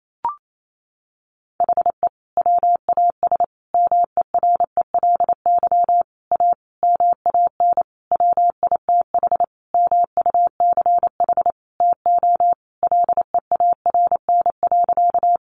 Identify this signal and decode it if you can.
Morse code, heard as a keyed tone, about 28 wpm, 700 Hz. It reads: HE WAS MERELY A MAN WITH MUCH TO LEARN.
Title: Sentences from Top 1400 Words 28wpm